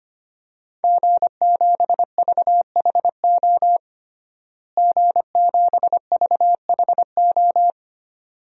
Morse code, heard as a keyed tone, about 25 wpm, 700 Hz. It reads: G7VHO G7VHO
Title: DXCC-100 Call Signs - 5 Characters - Repeated 2x - 25wpm